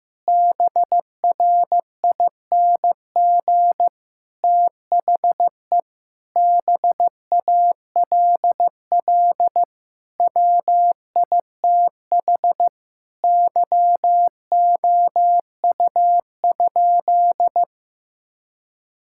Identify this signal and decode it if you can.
Morse code, heard as a keyed tone, about 15 wpm, 700 Hz. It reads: BRING THE BALL WITH YOU?
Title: Sentences from Top 1000 Words - Review Part 6 - 15wpm